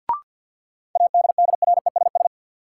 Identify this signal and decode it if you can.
Morse code, heard as a keyed tone, about 53 wpm, 700 Hz. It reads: ADDRESS
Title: Sentences from Top 1000 Words - New Word Drill - 50wpm